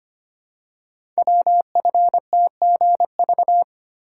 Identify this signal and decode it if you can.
Morse code, heard as a keyed tone, about 25 wpm, 700 Hz. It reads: WFTGV
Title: Code Groups - 5 Letters - 25wpm